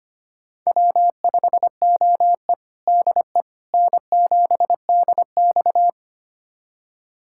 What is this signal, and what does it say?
Morse code, heard as a keyed tone, about 25 wpm, 700 Hz. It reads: W5OE DE N7DX
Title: USA Callsign DE Callsign - 4 Characters - 25wpm